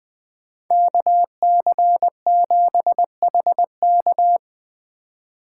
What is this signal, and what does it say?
Morse code, heard as a keyed tone, about 21 wpm, 700 Hz. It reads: KC7HK
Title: Speed Racing - USA 2x2 Call Signs 17wpm